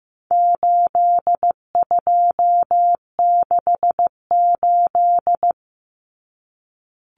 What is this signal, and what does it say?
Morse code, heard as a keyed tone, about 15 wpm, 700 Hz. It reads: 8268
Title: Numbers - 4-Digit 15wpm